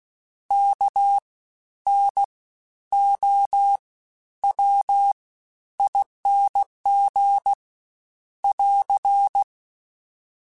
Morse code, heard as a keyed tone, about 16 wpm, spaced as for 10 wpm, 800 Hz. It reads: KNOWING <AR>
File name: Special INGs-mp3s / Special INGs-mp3s / SlowToFaster ing @15